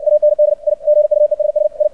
Morse code, heard as a keyed tone, about 59 wpm, 600 Hz. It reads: AND I GUESS I
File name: archived / CRSnet